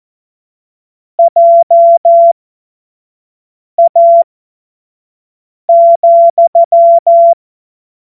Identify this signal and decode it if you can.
Morse code, heard as a keyed tone, about 14 wpm, spaced as for 5 wpm, 680 Hz. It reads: JA,
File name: Morsekurs / Witze / Witz 28.November 2023 25 F70